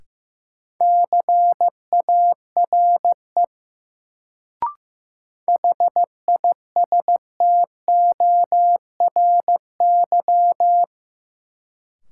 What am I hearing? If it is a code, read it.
Morse code, heard as a keyed tone, about 15 wpm, 700 Hz. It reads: CARE HISTORY
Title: Sets of 3 Words - New Word Drill - 15wpm